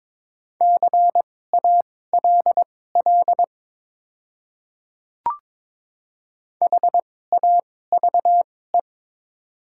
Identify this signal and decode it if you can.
Morse code, heard as a keyed tone, about 22 wpm, spaced as for 16 wpm, 700 Hz. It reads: CALL HAVE
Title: 2x Character Spacing - Top 500 Words - 22wpm